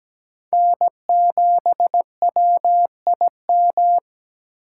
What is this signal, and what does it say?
Morse code, heard as a keyed tone, about 20 wpm, 700 Hz. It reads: N7WIM
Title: Speed Racing - USA 1x3 Call Signs 17wpm